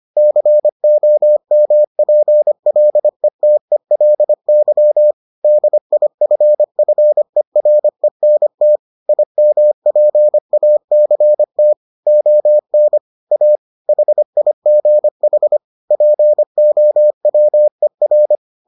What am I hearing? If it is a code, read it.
Morse code, heard as a keyed tone, about 25 wpm, 600 Hz. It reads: COMPLETELY DIFFERENT IMPACT ON A HIGH POWER